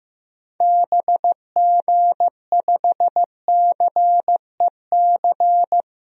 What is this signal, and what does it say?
Morse code, heard as a keyed tone, about 15 wpm, 700 Hz. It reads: BG5CEC
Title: DXCC-100 Call Signs - 6 Characters - Repeated 3x - 15wpm